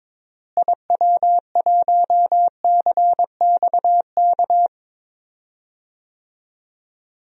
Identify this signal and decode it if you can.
Morse code, heard as a keyed tone, about 21 wpm, 700 Hz. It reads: IW1CXK
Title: DXCC-100 Call Signs - 6 Characters - Repeated 2x - 22wpm